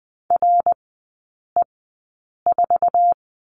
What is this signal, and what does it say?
Morse code, heard as a keyed tone, about 20 wpm, spaced as for 8 wpm, 700 Hz. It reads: RE4
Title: 067 Lesson 12e - Letter d - Three Characters 20wpm